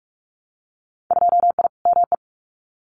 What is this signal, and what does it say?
Morse code, heard as a keyed tone, about 46 wpm, 690 Hz. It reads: 2I ME
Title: Winter Field Day Exchanges Encore 45wpm